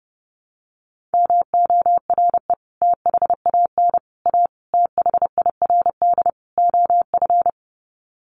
Morse code, heard as a keyed tone, about 28 wpm, 700 Hz. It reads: MORE THAN A THIRD OF
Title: Sets of 5 Words Encore2 30wpm